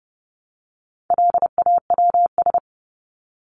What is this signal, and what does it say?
Morse code, heard as a keyed tone, about 30 wpm, 700 Hz. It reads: LAWS
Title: ES:S Suffix - Top 100 Words with Base Words 30wpm